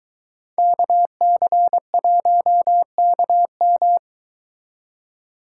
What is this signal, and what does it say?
Morse code, heard as a keyed tone, about 23 wpm, 700 Hz. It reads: KC1KM